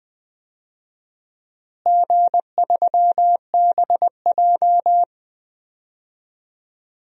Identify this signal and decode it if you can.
Morse code, heard as a keyed tone, about 20 wpm, 700 Hz. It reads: G3BJ
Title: DXCC-100 Call Signs - 4 Characters - Repeated 2x - 20wpm